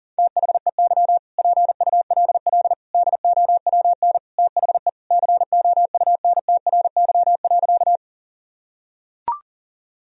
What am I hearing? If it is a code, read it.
Morse code, heard as a keyed tone, about 37 wpm, 700 Hz. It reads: THEY PULL DOWN THE COUNTRY.
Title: Sentences from Top 1000 Words - Review Part 2 - 40wpm